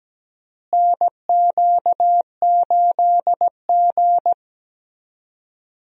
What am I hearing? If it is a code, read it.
Morse code, heard as a keyed tone, about 17 wpm, 700 Hz. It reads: NQ8G
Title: USA Call Signs - 4 Characters - Repeated 2x - 17wpm